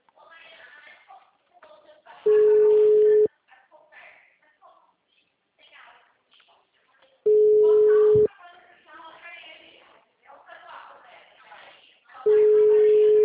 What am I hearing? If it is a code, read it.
Ringback tone (Europe, 425 Hz), with speech.
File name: out- -2008- - - .43765